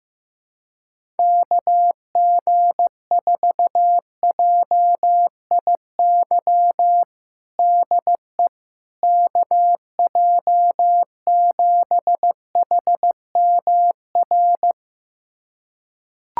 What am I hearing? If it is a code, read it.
Morse code, heard as a keyed tone, about 15 wpm, 700 Hz. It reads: KG4JIY DE KJ7HMR
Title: USA Callsign DE Callsign 15wpm